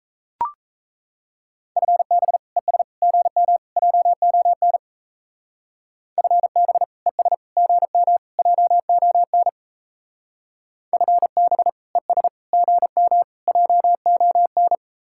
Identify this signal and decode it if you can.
Morse code, heard as a keyed tone, about 35 wpm, 700 Hz. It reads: FB ES GM JON FB ES GM JON FB ES GM JON
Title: Speed-Racing Common QSO Phrases Encore - 28wpm